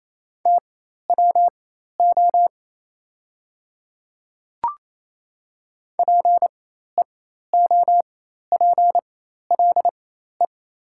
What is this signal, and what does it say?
Morse code, heard as a keyed tone, about 28 wpm, spaced as for 13 wpm, 700 Hz. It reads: TWO PEOPLE
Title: 4x Character Spacing - Top 100 Words - 28wpm